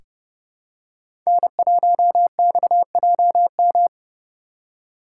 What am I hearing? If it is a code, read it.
Morse code, heard as a keyed tone, about 30 wpm, 700 Hz. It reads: N1XJM
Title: K1USN SST QSO Practice - 30wpm